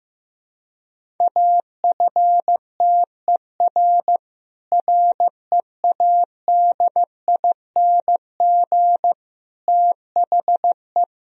Morse code, heard as a keyed tone, about 15 wpm, 700 Hz. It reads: AFTER READING THE
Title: Sentences from Top 1000 Words - Review Part 3 - 15wpm